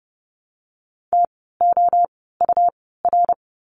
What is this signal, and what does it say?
Morse code, heard as a keyed tone, about 30 wpm, spaced as for 17 wpm, 700 Hz. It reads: TOUR
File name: Lesson 17f - Letter u - Words 30wpm